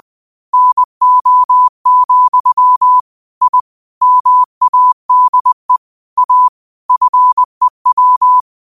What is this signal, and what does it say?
Morse code, heard as a keyed tone, about 20 wpm, 1000 Hz. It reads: NO, I MADE A FEW
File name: qotdfiles / 20wpm